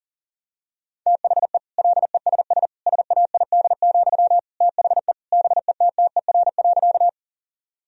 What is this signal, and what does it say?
Morse code, heard as a keyed tone, about 40 wpm, 700 Hz. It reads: THE LESS SAID, THE BETTER.
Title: Sentences from Top 1000 Words - Review Part 2 - 40wpm